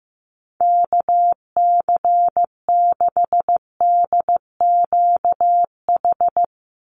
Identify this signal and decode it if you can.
Morse code, heard as a keyed tone, about 15 wpm, 700 Hz. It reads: KC6DQH
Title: USA Call Signs 15wpm